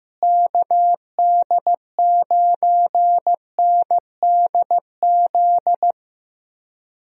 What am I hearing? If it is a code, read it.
Morse code, heard as a keyed tone, about 15 wpm, 700 Hz. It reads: KD9NDZ
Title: USA Call Signs - 6 Characters - Repeated 3x - 15wpm